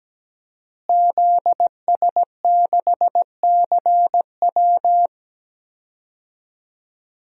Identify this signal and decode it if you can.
Morse code, heard as a keyed tone, about 17 wpm, 700 Hz. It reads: ZS6CW
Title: DXCC-100 Call Signs - 5 Characters - Repeated 2x - 17wpm